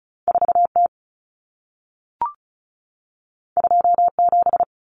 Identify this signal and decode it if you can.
Morse code, heard as a keyed tone, about 36 wpm, 700 Hz. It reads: 4T 27
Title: Common Cut Numbers - 2-Digit 35wpm